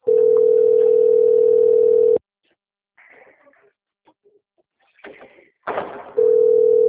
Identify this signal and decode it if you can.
Ringback tone (North America), 440 + 480 Hz.
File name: out- - - - - .77395